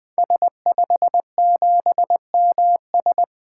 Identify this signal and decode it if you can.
Morse code, heard as a keyed tone, about 21 wpm, 700 Hz. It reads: S57MS
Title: 2023 SOTA Activator Call Signs - 20wpm